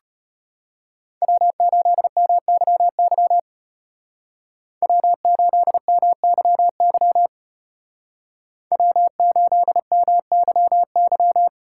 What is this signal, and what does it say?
Morse code, heard as a keyed tone, about 33 wpm, 700 Hz. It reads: W8MYY W8MYY W8MYY